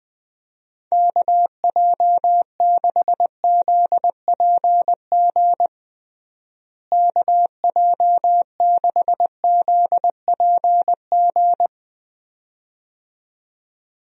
Morse code, heard as a keyed tone, about 20 wpm, 700 Hz. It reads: KJ6ZPG KJ6ZPG
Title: USA Call Signs - 6 Characters - Repeated 2x - 20wpm